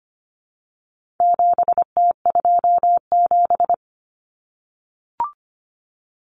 Morse code, heard as a keyed tone, about 25 wpm, 700 Hz. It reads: 7T27
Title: Common Cut Numbers - 4-Digit 25wpm